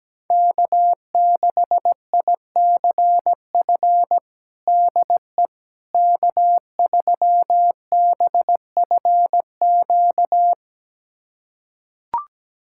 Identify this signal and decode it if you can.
Morse code, heard as a keyed tone, about 17 wpm, 700 Hz. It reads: K6ICF DE K3BFQ
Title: USA Callsign DE Callsign - 5 Characters - 17wpm